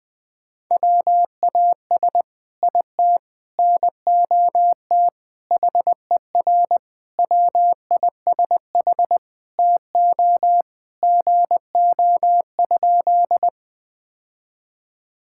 Morse code, heard as a keyed tone, about 20 wpm, 700 Hz. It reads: WAS IT NOT HER WISH TO GO?
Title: Sentences from Top 1600 Words 20wpm